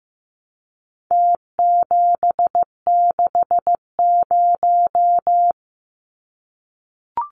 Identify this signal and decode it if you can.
Morse code, heard as a keyed tone, about 15 wpm, 700 Hz. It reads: T760
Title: Numbers - Contest Style - 4-Digit 15wpm